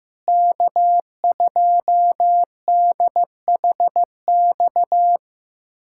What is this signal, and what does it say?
Morse code, heard as a keyed tone, about 15 wpm, 700 Hz. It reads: K2DHX
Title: USA Callsign DE Callsign - 5 Characters - 15wpm